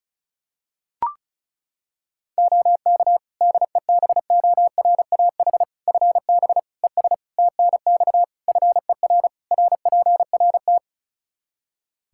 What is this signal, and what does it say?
Morse code, heard as a keyed tone, about 36 wpm, 700 Hz. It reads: OK DEBORAH FB ES TNX FER RPRT
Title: Practice QSO - Exchange 2 - Info - 35wpm